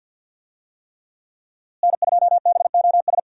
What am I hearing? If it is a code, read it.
Morse code, heard as a keyed tone, about 52 wpm, 700 Hz. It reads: N2BOS
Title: USA Call Signs - 5 Characters - Repeated 3x - 50wpm